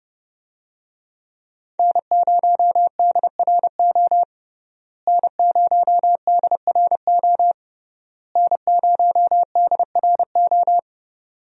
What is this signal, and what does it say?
Morse code, heard as a keyed tone, about 32 wpm, 700 Hz. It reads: N0DRO N0DRO N0DRO
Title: USA Call Signs - 5 Characters - Repeated 3x - 30wpm